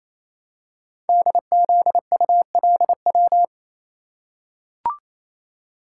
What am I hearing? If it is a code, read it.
Morse code, heard as a keyed tone, about 29 wpm, 700 Hz. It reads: DZULW
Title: Code Groups - 5 Letters - 28wpm